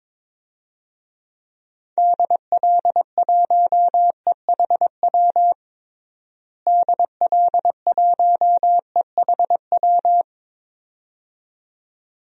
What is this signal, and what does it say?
Morse code, heard as a keyed tone, about 22 wpm, 700 Hz. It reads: DL1EHW DL1EHW
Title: DXCC-100 Call Signs - 6 Characters - Repeated 2x - 22wpm